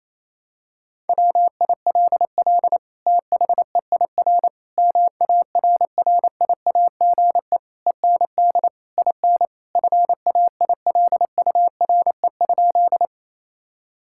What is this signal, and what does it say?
Morse code, heard as a keyed tone, about 28 wpm, 700 Hz. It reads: WILL THEIR MARRIAGE END IN FAILURE?
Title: Sentences from Top 1000 Words - Review Part 6 - 28wpm